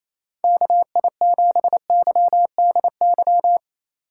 Morse code, heard as a keyed tone, about 28 wpm, 700 Hz. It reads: KI7YDY
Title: USA Call Signs - 6 Characters - 4x Character Spacing - 28wpm